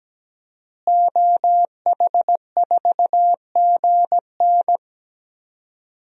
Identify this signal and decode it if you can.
Morse code, heard as a keyed tone, about 17 wpm, 700 Hz. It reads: OH4GN
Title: DXCC-100 Call Signs - 5 Characters - Repeated 3x - 17wpm